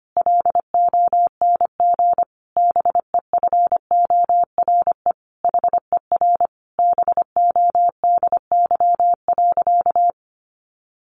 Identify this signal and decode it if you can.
Morse code, heard as a keyed tone, about 25 wpm, 700 Hz. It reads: LONG BEFORE HER BODY.
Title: Sentences from Top 1500 Words 25wpm